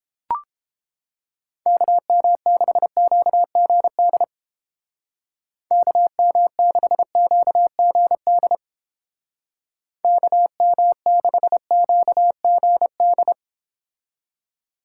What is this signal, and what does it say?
Morse code, heard as a keyed tone, about 31 wpm, 700 Hz. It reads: KM6QGD KM6QGD KM6QGD